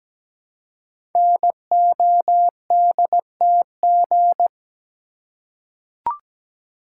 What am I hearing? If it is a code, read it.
Morse code, heard as a keyed tone, about 17 wpm, 700 Hz. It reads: NODTG
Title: Code Groups - 5 Letters - 17wpm